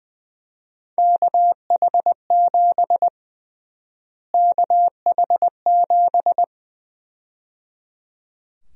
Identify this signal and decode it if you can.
Morse code, heard as a keyed tone, about 20 wpm, 700 Hz. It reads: KH7 KH7
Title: USA Call Sign Prefixes - Repeated 2x - 20wpm